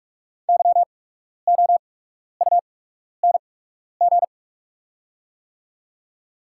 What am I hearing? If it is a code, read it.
Morse code, heard as a keyed tone, about 44 wpm, spaced as for 12 wpm, 690 Hz. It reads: YOUNG
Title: 8x Character Spacing - Top 500 Words - 45wpm